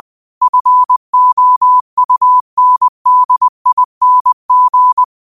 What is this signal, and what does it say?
Morse code, heard as a keyed tone, about 20 wpm, 1000 Hz. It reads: FOUNDING